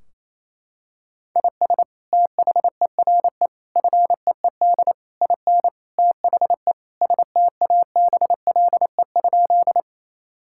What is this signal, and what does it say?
Morse code, heard as a keyed tone, about 28 wpm, 700 Hz. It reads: IS THERE FEED IN THE STABLE?
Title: Sentences from Top 1600 Words 28wpm